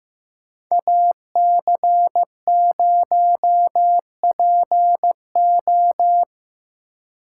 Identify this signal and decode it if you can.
Morse code, heard as a keyed tone, about 17 wpm, 700 Hz. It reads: AC0PO
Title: Speed Racing - USA 2x2 Call Signs 15wpm